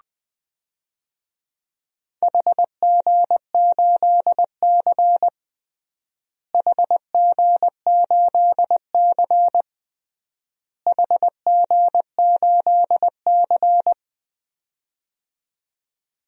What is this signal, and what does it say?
Morse code, heard as a keyed tone, about 21 wpm, 700 Hz. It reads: HG8C HG8C HG8C
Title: DXCC-100 Call Signs - 4 Characters - Repeated 3x - 20wpm